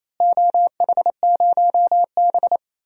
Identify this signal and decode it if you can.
Morse code, heard as a keyed tone, about 28 wpm, 700 Hz. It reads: OH0B
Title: DXCC-100 Call Signs - 4 Characters - Repeated 3x - 28wpm